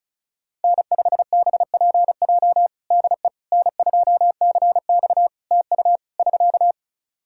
Morse code, heard as a keyed tone, about 36 wpm, 700 Hz. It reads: N5BPJ DE N2CX TU <SK>
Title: K1USN SST QSO Practice 35wpm